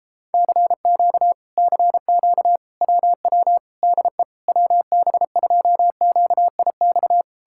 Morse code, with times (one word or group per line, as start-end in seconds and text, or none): 0.3-1.3 CQ
1.6-2.6 CQ
2.8-3.6 WW
3.8-4.2 DE
4.5-7.2 WB2QIX